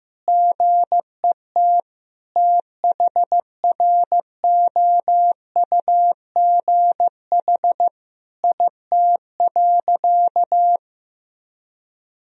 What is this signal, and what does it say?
Morse code, heard as a keyed tone, about 15 wpm, 700 Hz. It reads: GET THROUGH IT.
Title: Sentences from Top 1000 Words - Review Part 2 - 15wpm